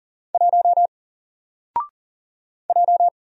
Morse code, heard as a keyed tone, about 40 wpm, 700 Hz. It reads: 1 J
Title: Single Character (Letter:Number:Punctuation:Prosign) 40wpm